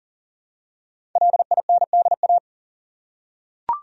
Morse code, heard as a keyed tone, about 40 wpm, 700 Hz. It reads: LINDA
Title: USA Common Names - 40wpm